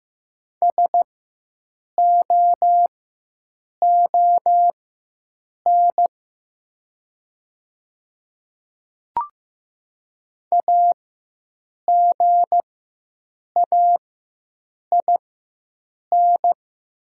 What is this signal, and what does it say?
Morse code, heard as a keyed tone, about 15 wpm, spaced as for 7 wpm, 700 Hz. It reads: SOON AGAIN
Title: 4x Character Spacing - Top 500 Words - 15wpm